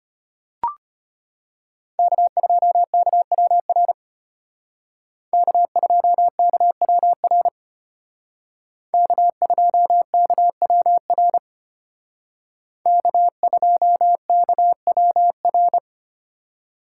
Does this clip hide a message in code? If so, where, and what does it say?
Yes: Morse code from 2.0 to 15.8 s, heard as a keyed tone, about 32 wpm, 700 Hz. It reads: K2KWR K2KWR K2KWR K2KWR